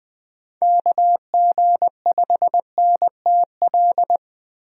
Morse code, heard as a keyed tone, about 20 wpm, 700 Hz. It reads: KG5NTL
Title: USA Call Signs - 6 Characters - 20wpm